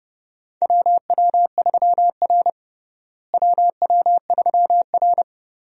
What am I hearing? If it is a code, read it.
Morse code, heard as a keyed tone, about 29 wpm, 700 Hz. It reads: WW3R WW3R
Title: USA Call Signs - 4 Characters - Repeated 2x - 30wpm